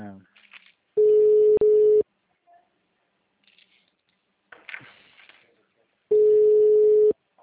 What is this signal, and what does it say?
Ringback tone (Europe, 425 Hz), with speech.